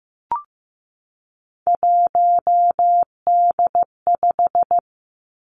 Morse code, heard as a keyed tone, about 15 wpm, 700 Hz. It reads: 1D5
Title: Three Challenging Letter-Number Combinations 15wpm